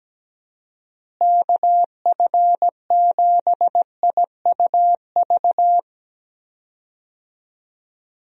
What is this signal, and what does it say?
Morse code, heard as a keyed tone, about 17 wpm, 700 Hz. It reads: KF7IUV
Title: USA Call Signs - 6 Characters - Repeated 2x - 17wpm